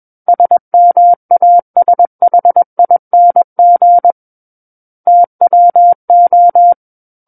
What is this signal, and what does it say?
Morse code, heard as a keyed tone, about 21 wpm, 700 Hz. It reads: SMASHING TWO
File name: Book One / 12-Chapter 12 - 21 WPM